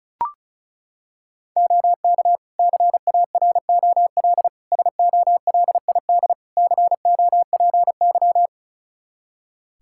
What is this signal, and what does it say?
Morse code, heard as a keyed tone, about 34 wpm, 700 Hz. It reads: OK CAROL SOLID COPY
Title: K1USN SST QSO Practice 35wpm